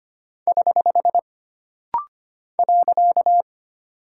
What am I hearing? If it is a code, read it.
Morse code, heard as a keyed tone, about 25 wpm, 700 Hz. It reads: <HH> .
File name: Single Punctuation-Prosign - Rapid-Fire - 25wpm